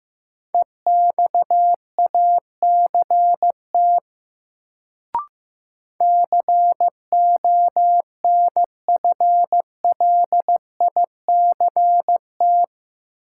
Morse code, heard as a keyed tone, about 15 wpm, 700 Hz. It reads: EXACT CONFLICT
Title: Sentences from Top 1300 Words - New Word Drill Alt - 15wpm